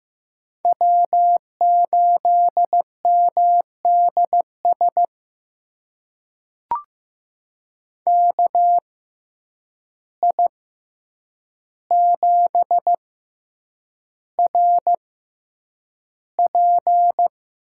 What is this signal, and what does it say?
Morse code, heard as a keyed tone, about 15 wpm, spaced as for 8 wpm, 700 Hz. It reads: W8MDS KI7RP
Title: USA Call Signs - 6x Character Spacing - 15wpm